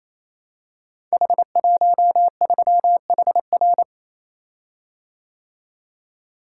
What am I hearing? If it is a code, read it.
Morse code, heard as a keyed tone, about 27 wpm, 700 Hz. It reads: H13HR
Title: Code Groups - 5 Letter:Numbers - 28wpm